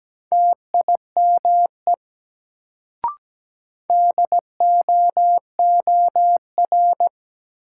Morse code, heard as a keyed tone, about 17 wpm, 700 Hz. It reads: TIME DOOR
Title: Sets of 2 Words - New Word Drill - 17wpm